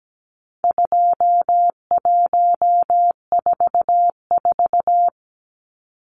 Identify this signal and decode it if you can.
Morse code, heard as a keyed tone, about 17 wpm, 700 Hz. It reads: 2144
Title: Numbers - 4-Digit 17wpm